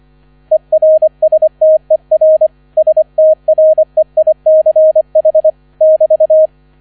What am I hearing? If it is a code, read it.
Morse code, heard as a keyed tone, about 25 wpm, 630 Hz. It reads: ERSTER STREICH <BT>